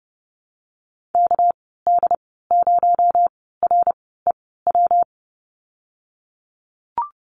Morse code, heard as a keyed tone, about 30 wpm, spaced as for 17 wpm, 700 Hz. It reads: KD0REW
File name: Lesson 37g - Letter x - Call Signs 30wpm